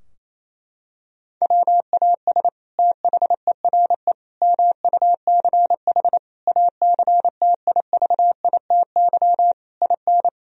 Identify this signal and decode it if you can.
Morse code, heard as a keyed tone, about 27 wpm, 700 Hz. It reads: WAS THERE MUCH ACTIVITY IN
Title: Sentences from Top 1000 Words - Review Part 4 - 28wpm